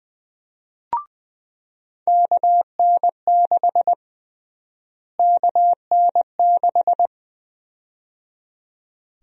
Morse code, heard as a keyed tone, about 20 wpm, 700 Hz. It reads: KN6 KN6
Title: USA Call Sign Prefixes - Repeated 2x - 20wpm